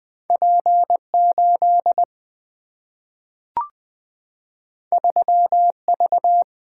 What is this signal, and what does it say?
Morse code, heard as a keyed tone, about 20 wpm, 700 Hz. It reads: P 8 3 V
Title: Single-Letter Single-Number 20wpm